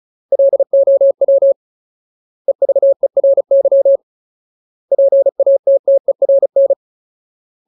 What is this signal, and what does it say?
Morse code, heard as a keyed tone, about 34 wpm, 550 Hz. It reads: LOW EVERY PATTERN
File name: Ultimate Word List / 35WPM Ultimate